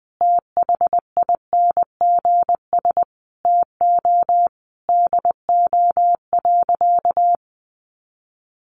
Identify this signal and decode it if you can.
Morse code, heard as a keyed tone, about 19 wpm, 700 Hz. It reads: THINGS TO DO.
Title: Additional Easy Phrases 20wpm